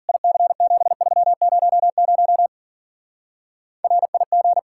Morse code, heard as a keyed tone, about 48 wpm, 700 Hz. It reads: IC7300 RIG
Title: Speed-Racing Common QSO Phrases - 40wpm